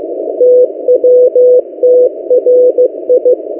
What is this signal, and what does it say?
Morse code, heard as a keyed tone, about 15 wpm, 500 Hz. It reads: TWTRI
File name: M08a- -1355utc-8096khz